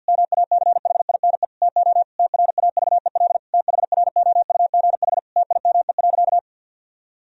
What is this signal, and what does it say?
Morse code, heard as a keyed tone, about 52 wpm, 700 Hz. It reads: MACHINE TO TRAVEL THROUGH TIME.
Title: Sentences from Top 1000 Words - Review Part 2 - 50wpm